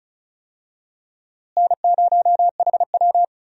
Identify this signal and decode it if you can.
Morse code, heard as a keyed tone, about 34 wpm, 700 Hz. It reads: N0HW
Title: USA Call Signs - 4 Characters - Repeated 2x - 35wpm